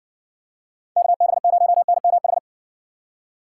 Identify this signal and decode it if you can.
Morse code, heard as a keyed tone, about 57 wpm, 700 Hz. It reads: KB0NMH